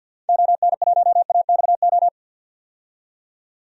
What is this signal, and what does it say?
Morse code, heard as a keyed tone, about 46 wpm, 700 Hz. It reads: ON1AXO
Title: DXCC-100 Call Signs - 6 Characters - 50wpm